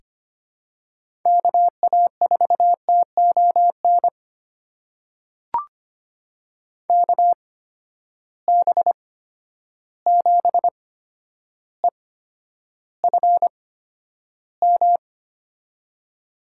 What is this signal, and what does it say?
Morse code, heard as a keyed tone, about 25 wpm, spaced as for 11 wpm, 700 Hz. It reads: KA4TON KB7EFM
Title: USA Call Signs - 6 Characters - 8x Character Spacing - 25wpm